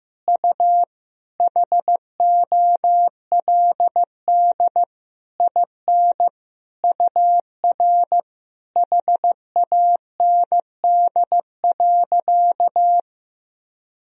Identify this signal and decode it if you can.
Morse code, heard as a keyed tone, about 16 wpm, 700 Hz. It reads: U HOLD IN UR HAND.
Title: Sentences from Top 400 Words and Abbreviations 15wpm